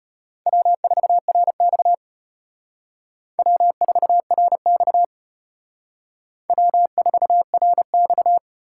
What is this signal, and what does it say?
Morse code, heard as a keyed tone, about 32 wpm, 700 Hz. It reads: W4RX W4RX W4RX